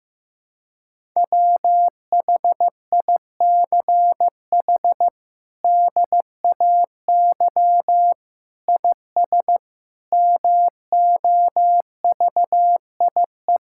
Morse code, heard as a keyed tone, about 15 wpm, 700 Hz. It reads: WHICH DAY IS MOVIE
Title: Sentences from Top 1000 Words - Review Part 5 - 15wpm